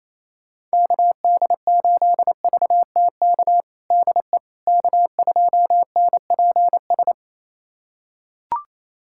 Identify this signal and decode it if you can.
Morse code, heard as a keyed tone, about 27 wpm, 700 Hz. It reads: KD8VTK DE K2NPS
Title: USA Callsign DE Callsign 28wpm